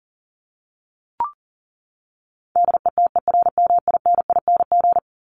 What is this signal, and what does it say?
Morse code, heard as a keyed tone, about 40 wpm, 700 Hz. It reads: DETERMINING
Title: ING Suffix - Top 100 Words with Base Words 40wpm